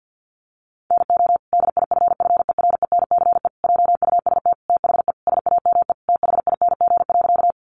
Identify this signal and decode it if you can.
Morse code, heard as a keyed tone, about 50 wpm, 700 Hz. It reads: NO DIFFERENCE JUST THE SAME THING.
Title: Sentences from Top 800 Words 50wpm